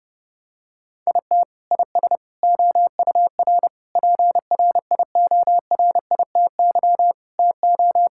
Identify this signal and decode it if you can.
Morse code, heard as a keyed tone, about 30 wpm, 700 Hz. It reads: IT IS OUR PRIORITY TO
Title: Sentences from Top 1600 Words 30wpm